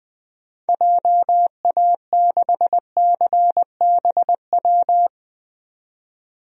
Morse code, heard as a keyed tone, about 20 wpm, 700 Hz. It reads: JA6CBW
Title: DXCC-100 Call Signs - 6 Characters - 20wpm